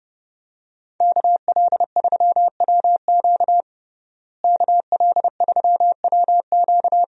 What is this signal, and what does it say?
Morse code, heard as a keyed tone, about 30 wpm, 700 Hz. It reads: KL3WQ KL3WQ
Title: USA Call Signs - 5 Characters - Repeated 3x - 30wpm